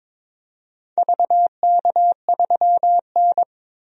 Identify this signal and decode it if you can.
Morse code, heard as a keyed tone, about 23 wpm, 700 Hz. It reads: VK3N
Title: DXCC-100 Call Signs - 4 Characters - 22wpm